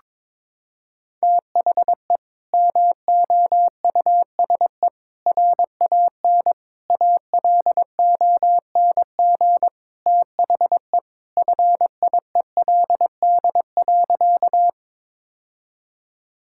Morse code, heard as a keyed tone, about 22 wpm, 700 Hz. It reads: THE MOUSE RAN ALONG THE FIELD.
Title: Sentences from Top 1000 Words - Review Part 5 - 22wpm